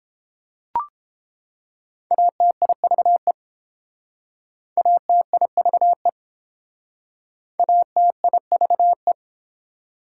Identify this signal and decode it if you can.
Morse code, heard as a keyed tone, about 30 wpm, 700 Hz. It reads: ATIVE ATIVE ATIVE